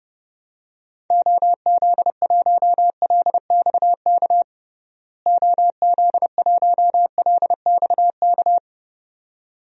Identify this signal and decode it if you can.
Morse code, heard as a keyed tone, about 32 wpm, 700 Hz. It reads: OZ1LXK OZ1LXK
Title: DXCC-100 Call Signs - 6 Characters - Repeated 2x - 30wpm